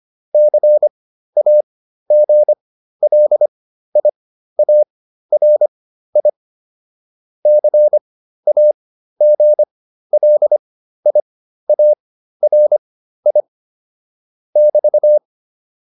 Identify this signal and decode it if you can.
Morse code, heard as a keyed tone, about 25 wpm, spaced as for 13 wpm, 600 Hz. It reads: CAGLIARI CAGLIARI <BT>